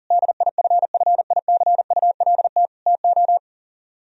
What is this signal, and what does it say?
Morse code, heard as a keyed tone, about 50 wpm, 700 Hz. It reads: DIFFICULT TO